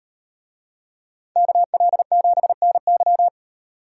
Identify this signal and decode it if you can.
Morse code, heard as a keyed tone, about 38 wpm, 700 Hz. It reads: KL7NY